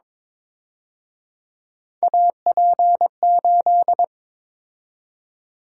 Morse code, heard as a keyed tone, about 27 wpm, 700 Hz. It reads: AP8
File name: Speed-Racing - Three Letter-Number Combinations - 22wpm